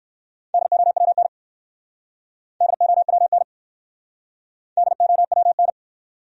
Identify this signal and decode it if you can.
Morse code, heard as a keyed tone, about 60 wpm, 700 Hz. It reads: DOWN DOWN DOWN